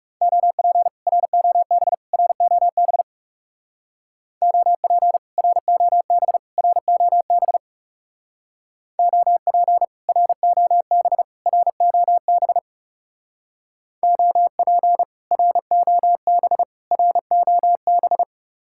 Morse code, heard as a keyed tone, about 36 wpm, 700 Hz. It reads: OP ROB ROB OP ROB ROB OP ROB ROB OP ROB ROB
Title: Speed-Racing Common QSO Phrases - 30wpm